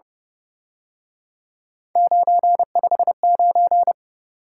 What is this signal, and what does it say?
Morse code, heard as a keyed tone, about 30 wpm, 700 Hz. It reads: 959
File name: Speed-Racing - Three Letter-Number Combinations - 25wpm